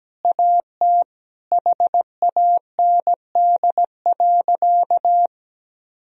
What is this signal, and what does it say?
Morse code, heard as a keyed tone, about 17 wpm, 700 Hz. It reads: AT HAND.
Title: Sentences from Top 1000 Words - Review Part 1 - 17wpm